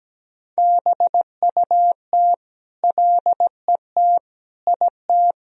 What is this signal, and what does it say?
Morse code, heard as a keyed tone, about 17 wpm, 700 Hz. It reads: BUT LET IT
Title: Sentences from Top 1000 Words - Review Part 1 - 17wpm